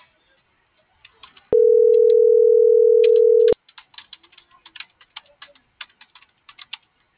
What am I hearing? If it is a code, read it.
Ringback tone (North America), 440 + 480 Hz.